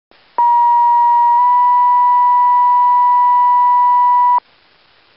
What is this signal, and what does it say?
Two-tone page, heard as a one-second tone, then a three-second tone, about 950 then 980 Hz. It is heard from 0.4 to 4.4 s.